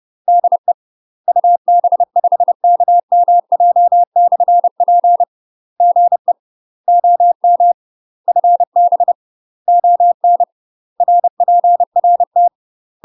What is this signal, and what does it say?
Morse code, heard as a keyed tone, about 30 wpm, 700 Hz. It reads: DE UB5KMJ/P GE OM FB ON RPRT
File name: UK MASTER QSOs TRAINING / 30WPM / UK MASTER QSOs 0042